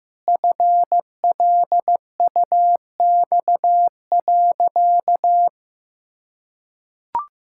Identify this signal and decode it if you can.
Morse code, heard as a keyed tone, about 15 wpm, 700 Hz. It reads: FLUX.
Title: Pangram Challenge v2 - 15wpm